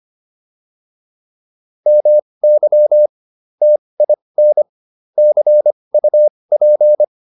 Morse code, heard as a keyed tone, about 25 wpm, spaced as for 20 wpm, 600 Hz. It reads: MY TIN CUP